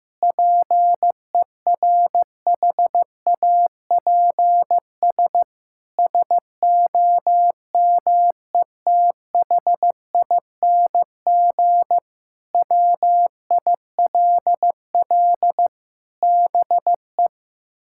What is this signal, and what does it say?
Morse code, heard as a keyed tone, about 15 wpm, 700 Hz. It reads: PERHAPS SOMETHING WILL BE
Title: Sentences from Top 1600 Words 15wpm